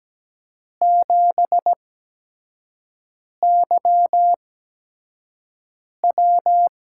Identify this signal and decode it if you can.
Morse code, heard as a keyed tone, about 17 wpm, spaced as for 5 wpm, 700 Hz. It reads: 7YW
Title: USA Call Signs - 8x Character Spacing - 17wpm